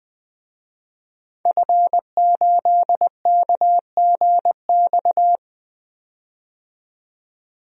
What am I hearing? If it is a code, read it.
Morse code, heard as a keyed tone, about 21 wpm, 700 Hz. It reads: F8KGX
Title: DXCC-100 Call Signs - 5 Characters - 20wpm